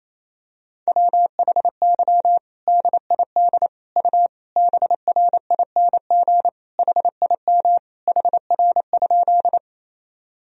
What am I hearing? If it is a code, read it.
Morse code, heard as a keyed tone, about 28 wpm, 700 Hz. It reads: WHY DID U BRING HIM HR?
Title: Sentences from Top 500 Words and Abbreviations 28wpm